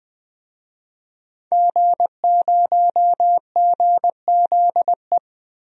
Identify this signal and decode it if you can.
Morse code, heard as a keyed tone, about 21 wpm, 700 Hz. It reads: G0GZE
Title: DXCC-100 Call Signs - 5 Characters - Repeated 3x - 20wpm